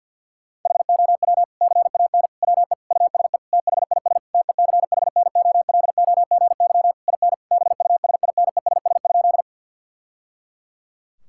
Morse code, heard as a keyed tone, about 52 wpm, 700 Hz. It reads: HOW CAN WE USE THIS TECHNOLOGY IN BUSINESS?
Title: Sentences from Top 1000 Words - Review Part 5 - 50wpm